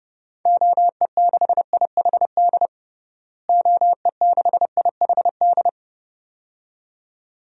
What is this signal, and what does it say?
Morse code, heard as a keyed tone, about 30 wpm, 700 Hz. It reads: OE6IHD OE6IHD
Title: DXCC-100 Call Signs - 6 Characters - Repeated 2x - 30wpm